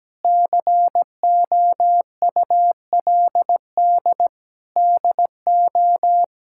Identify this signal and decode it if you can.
Morse code, heard as a keyed tone, about 17 wpm, 700 Hz. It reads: COULD DO
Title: Sentences from Top 1600 Words 17wpm